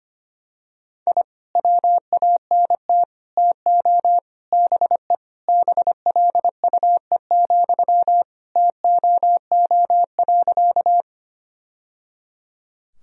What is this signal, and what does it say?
Morse code, heard as a keyed tone, about 25 wpm, 700 Hz. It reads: I WANT TO BE BLUE, TOO.
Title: Sentences from Top 1000 Words - Review Part 3 - 25wpm